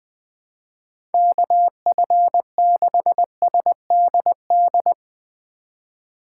Morse code, heard as a keyed tone, about 20 wpm, 700 Hz. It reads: KF6SDD
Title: USA Call Signs - 6 Characters - 3x Character Spacing - 20wpm